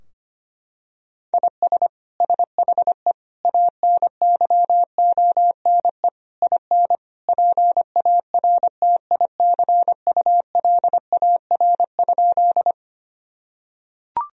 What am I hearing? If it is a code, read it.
Morse code, heard as a keyed tone, about 25 wpm, 700 Hz. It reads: IS SHE ANYONE IN PARTICULAR?
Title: Sentences from Top 1000 Words - Review Part 4 - 25wpm